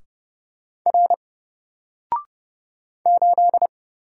Morse code, heard as a keyed tone, about 30 wpm, 700 Hz. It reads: R 8
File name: Single-Letter-Number-30wpm